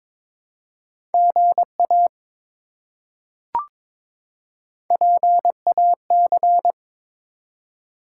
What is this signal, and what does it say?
Morse code, heard as a keyed tone, about 22 wpm, 700 Hz. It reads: GA PAC
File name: ARRL Sections v2 - 22wpm